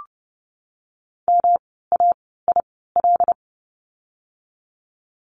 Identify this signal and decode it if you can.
Morse code, heard as a keyed tone, about 30 wpm, spaced as for 17 wpm, 700 Hz. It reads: MAIL
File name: course / 30:10wpm / 108 Lesson 18f - Letter m - Words 30wpm